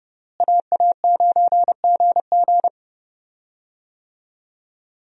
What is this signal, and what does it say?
Morse code, heard as a keyed tone, about 30 wpm, 700 Hz. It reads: AA9GG
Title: USA Call Signs - 5 Characters - Repeated 2x - 30wpm